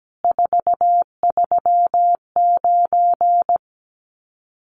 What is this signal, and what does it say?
Morse code, heard as a keyed tone, about 17 wpm, 700 Hz. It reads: 439
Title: Common QSO Phrases 17wpm